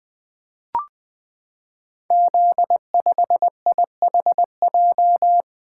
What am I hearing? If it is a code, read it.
Morse code, heard as a keyed tone, about 20 wpm, 700 Hz. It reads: Z5IHJ
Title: Code Groups - 5 Letter:Numbers - 20wpm